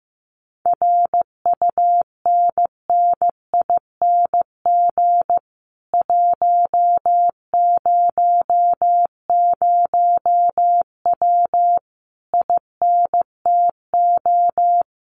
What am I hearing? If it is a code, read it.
Morse code, heard as a keyed tone, about 15 wpm, 700 Hz. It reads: RUNNING 100W INTO
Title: Common QSO Phrases Encore 15wpm